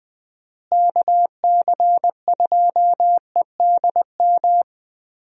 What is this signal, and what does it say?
Morse code, heard as a keyed tone, about 20 wpm, 700 Hz. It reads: KC2EDM
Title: USA Call Signs - 6 Characters - Repeated 2x - 20wpm